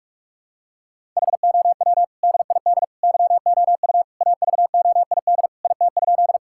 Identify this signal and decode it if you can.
Morse code, heard as a keyed tone, about 46 wpm, 700 Hz. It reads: HOW DID YOU AVOID IT?
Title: Sentences from Top 1000 Words - Review Part 6 - 45wpm